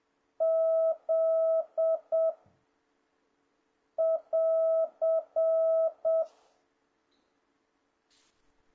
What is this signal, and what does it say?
Morse code, heard as a keyed tone, about 7 wpm, 650 Hz. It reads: Z <AR>
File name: jh8cbh / 2026cw